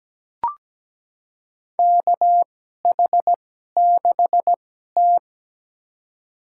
Morse code, heard as a keyed tone, about 17 wpm, spaced as for 12 wpm, 700 Hz. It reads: KH6T
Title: USA Call Signs - 4 Characters - 2x Character Spacing - 17wpm